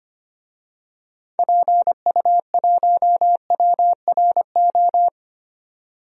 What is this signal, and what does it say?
Morse code, heard as a keyed tone, about 25 wpm, 700 Hz. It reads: PU1WRO
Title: DXCC-100 Call Signs - 6 Characters - 25wpm